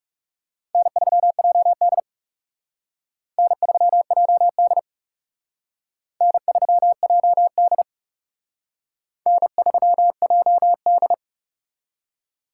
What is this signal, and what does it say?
Morse code, heard as a keyed tone, about 37 wpm, 700 Hz. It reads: N3JD N3JD N3JD N3JD
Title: Speed Racing - USA 1x2 Call Signs 30wpm